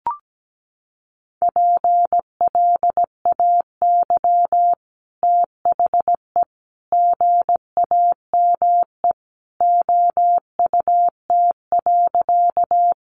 Morse code, heard as a keyed tone, about 17 wpm, 700 Hz. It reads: PLAY THE GAME OUT.
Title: Sentences from Top 500 Words 17wpm